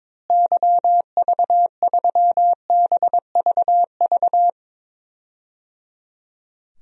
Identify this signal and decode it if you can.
Morse code, heard as a keyed tone, about 22 wpm, 700 Hz. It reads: YV3BVV
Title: DXCC-100 Call Signs - 6 Characters - Repeated 3x - 22wpm